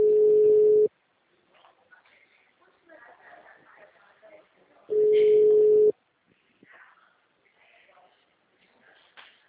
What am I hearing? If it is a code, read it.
Ringback tone (Europe, 425 Hz), with speech.